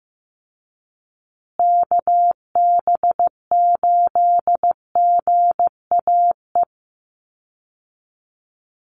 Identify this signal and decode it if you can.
Morse code, heard as a keyed tone, about 14 wpm, 700 Hz. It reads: KB8GAE
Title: Call Signs Repeated 2x - 15wpm